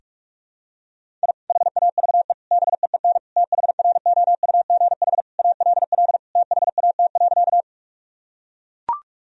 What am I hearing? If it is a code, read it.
Morse code, heard as a keyed tone, about 41 wpm, 700 Hz. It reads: I HAVE BEEN THROUGH ALL THAT.
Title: Sentences from Top 1000 Words - Review Part 1 - 45wpm